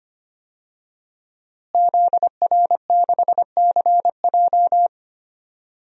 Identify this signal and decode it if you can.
Morse code, heard as a keyed tone, about 25 wpm, 700 Hz. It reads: ZR6CJ
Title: DXCC-100 Call Signs - 5 Characters - Repeated 2x - 25wpm